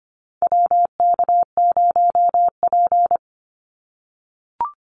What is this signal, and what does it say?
Morse code, heard as a keyed tone, about 25 wpm, 700 Hz. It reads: WK0P
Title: Call Signs - 25wpm